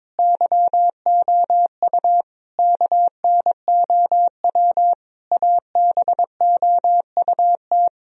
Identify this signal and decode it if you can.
Morse code, heard as a keyed tone, about 22 wpm, 700 Hz. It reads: YOU KNOW ABOUT
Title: Sentences from Top 1000 Words - Review Part 4 - 22wpm